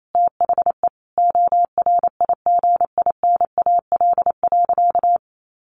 Morse code, heard as a keyed tone, about 28 wpm, 700 Hz. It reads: THE ORIGINAL.
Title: Sentences from Top 1500 Words 28wpm